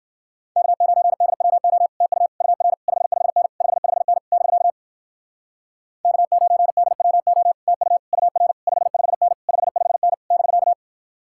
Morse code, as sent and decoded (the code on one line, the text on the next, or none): -.- ----. -.. .-- --- / - ..- / ..- .-. / ..... ..... -. / ..... ..... -. / -...-.- / -.- ----. -.. .-- --- / - ..- / ..- .-. / ..... ..... -. / ..... ..... -. / -...-.-
K9DWO TU UR 55N 55N <BK> K9DWO TU UR 55N 55N <BK>